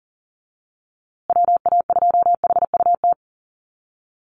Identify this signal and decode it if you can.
Morse code, heard as a keyed tone, about 40 wpm, 700 Hz. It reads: WA2HUT
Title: USA Call Signs Repeated 2x 40wpm